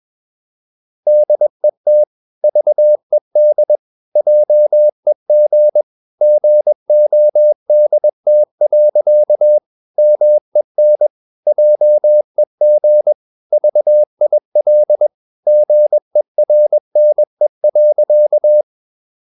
Morse code, heard as a keyed tone, about 21 wpm, 600 Hz. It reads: DET VED JEG GODT. MEN JEG VIL GERNE.
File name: Ferie QRS 21wpm